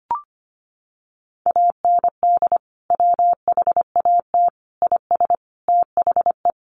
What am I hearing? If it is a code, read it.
Morse code, heard as a keyed tone, about 25 wpm, 700 Hz. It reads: AND WHAT IS THE
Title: Sentences from Top 700 Words 25wpm